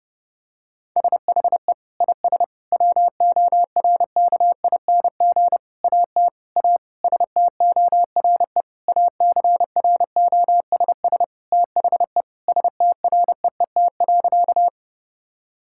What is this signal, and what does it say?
Morse code, heard as a keyed tone, about 30 wpm, 700 Hz. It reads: SHE IS WORKING AT A STORE ACROSS THE STREET.
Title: Sentences from Top 1000 Words - Review Part 5 - 30wpm